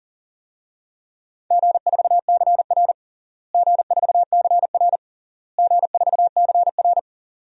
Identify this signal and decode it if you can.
Morse code, heard as a keyed tone, about 40 wpm, 700 Hz. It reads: G4CR G4CR G4CR
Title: DXCC-100 Call Signs - 4 Characters - Repeated 3x - 40wpm